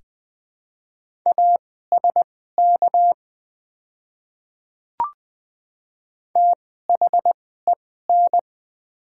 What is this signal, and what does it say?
Morse code, heard as a keyed tone, about 20 wpm, spaced as for 14 wpm, 700 Hz. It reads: ASK THEN
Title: 2x Character Spacing - Top 200 Words - 20wpm